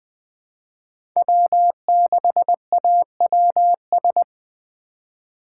Morse code, heard as a keyed tone, about 20 wpm, 700 Hz. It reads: W6AWS
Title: K1USN SST QSO Practice - 20wpm